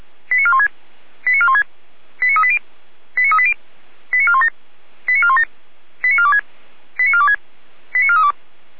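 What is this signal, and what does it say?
Five-tone selective call (ZVEI), heard as quick runs of five tones, tones of about 70 ms. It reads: 89416 89416 89380 89380 89417 89417 89426 89426 89432
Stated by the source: ZVEI five-tone call received by SDR.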